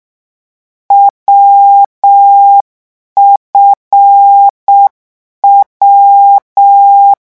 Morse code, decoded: WFW